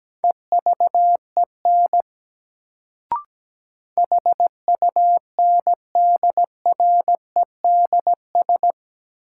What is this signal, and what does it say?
Morse code, heard as a keyed tone, about 17 wpm, 700 Hz. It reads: EVEN HUNDREDS
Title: Sets of 4 Words - New Word Drill - 17wpm